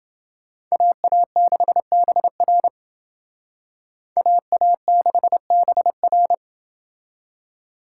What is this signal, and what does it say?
Morse code, heard as a keyed tone, about 28 wpm, 700 Hz. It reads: AA6BR AA6BR